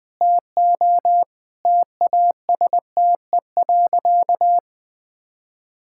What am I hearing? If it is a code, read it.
Morse code, heard as a keyed tone, about 20 wpm, 700 Hz. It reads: TO TASTE.
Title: Sentences with 5-Word Phrases - 20wpm